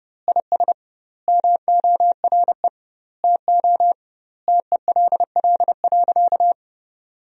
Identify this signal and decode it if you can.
Morse code, heard as a keyed tone, about 30 wpm, 700 Hz. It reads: IS MORE TO TELL.
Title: 2x Word-Spacing - Sentences from Top 200 Words - 30wpm